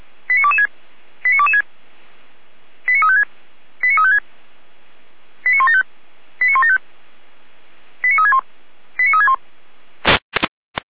Five-tone selective call (ZVEI), heard as quick runs of five tones, tones of about 80 ms. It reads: 89226 89226 89356 89356 89175 89175 89361 89361